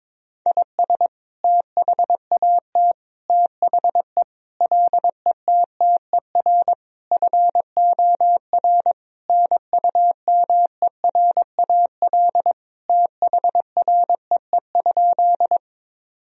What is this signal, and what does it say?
Morse code, heard as a keyed tone, about 22 wpm, 700 Hz. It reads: IS THAT THE LETTER FOR NUMERAL THREE?
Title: Sentences from Top 1000 Words - Review Part 2 - 22wpm